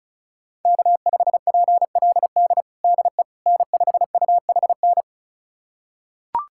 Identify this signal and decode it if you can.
Morse code, heard as a keyed tone, about 34 wpm, 700 Hz. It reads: K5PLD DE N5UHN
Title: USA Callsign DE Callsign - 5 Characters - 35wpm